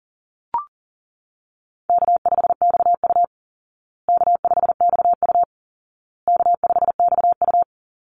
Morse code, heard as a keyed tone, about 40 wpm, 700 Hz. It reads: K5XU K5XU K5XU
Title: Easy Call Signs Repeated 3x - 40wpm